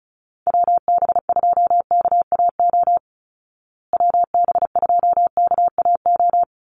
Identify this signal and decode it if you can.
Morse code, heard as a keyed tone, about 34 wpm, 700 Hz. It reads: WB2KAO WB2KAO
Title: Call Signs Repeated 3x - 35wpm